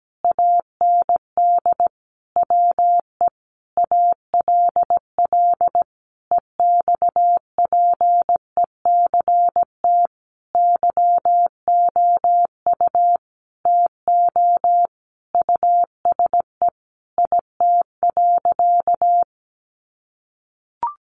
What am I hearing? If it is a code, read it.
Morse code, heard as a keyed tone, about 17 wpm, 700 Hz. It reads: AND WE ALL EXPECT YOU TO USE IT.
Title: Sentences from Top 1100 Words 17wpm